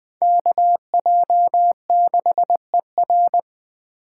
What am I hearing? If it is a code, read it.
Morse code, heard as a keyed tone, about 20 wpm, 700 Hz. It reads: KJ6ER
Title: 2023 POTA Activator Call Signs - 20wpm